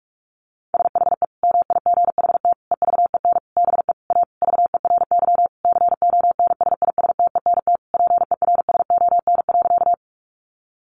Morse code, heard as a keyed tone, about 44 wpm, 700 Hz. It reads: SHE MIGHT EVEN BE A VERY CONSISTENT PERSON.
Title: Sentences from Top 1500 Words 45wpm